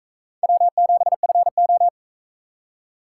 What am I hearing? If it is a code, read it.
Morse code, heard as a keyed tone, about 44 wpm, 700 Hz. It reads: W7FO